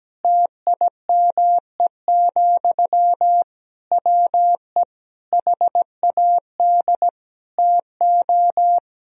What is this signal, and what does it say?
Morse code, heard as a keyed tone, about 17 wpm, 700 Hz. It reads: TIME, WE HAD TO
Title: Sentences from Top 1000 Words - Review Part 2 - 17wpm